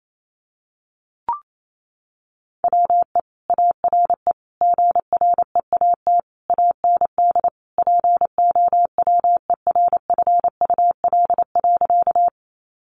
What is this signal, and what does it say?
Morse code, heard as a keyed tone, about 28 wpm, 700 Hz. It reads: WE ARE GREAT AND POWERFUL.
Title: Sentences from Top 1300 Words 28wpm